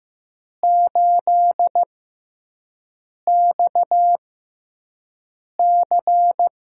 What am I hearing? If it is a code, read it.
Morse code, heard as a keyed tone, about 15 wpm, spaced as for 5 wpm, 700 Hz. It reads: 8XC
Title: USA Call Signs - 6x Character Spacing - 15wpm